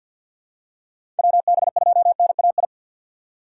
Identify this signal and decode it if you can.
Morse code, heard as a keyed tone, about 52 wpm, 690 Hz. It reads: WB2NAI
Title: USA Call Signs - 2x Character Spacing - 50wpm